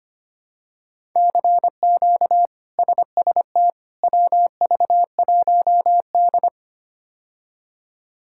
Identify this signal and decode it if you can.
Morse code, heard as a keyed tone, about 25 wpm, 700 Hz. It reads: CQ SST WV1D
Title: K1USN SST QSO Practice - 25wpm